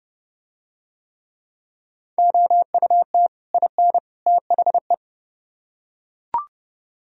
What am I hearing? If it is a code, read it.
Morse code, heard as a keyed tone, about 29 wpm, 700 Hz. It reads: OUT IN THE